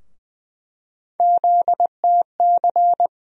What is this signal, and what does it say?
Morse code, heard as a keyed tone, about 20 wpm, 700 Hz. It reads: ZTC
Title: USA Call Sign Postfixes - 3 Letters 20wpm